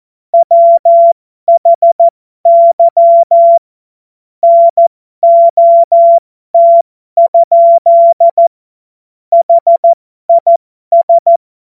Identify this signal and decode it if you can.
Morse code, heard as a keyed tone, about 14 wpm, spaced as for 12 wpm, 680 Hz. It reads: WHY NOT? HIS
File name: Morsekurs / Witze